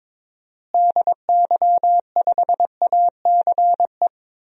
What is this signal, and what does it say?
Morse code, heard as a keyed tone, about 21 wpm, 700 Hz. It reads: DY5ACE
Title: DXCC-100 Call Signs - 6 Characters - Repeated 2x - 22wpm